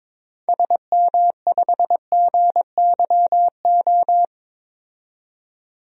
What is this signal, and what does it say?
Morse code, heard as a keyed tone, about 22 wpm, 700 Hz. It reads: SM5GYO
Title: DXCC-100 Call Signs - 6 Characters - 22wpm